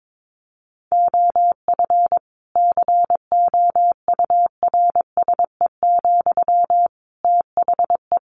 Morse code, heard as a keyed tone, about 22 wpm, 700 Hz. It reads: OF COURSE, THE
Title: Sentences from Top 1500 Words 22wpm